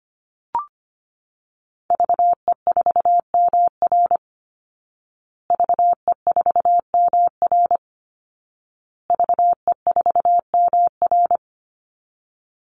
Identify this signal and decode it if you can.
Morse code, heard as a keyed tone, about 25 wpm, 700 Hz. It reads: VE4MR VE4MR VE4MR
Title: Call Signs Repeated 2x - 25wpm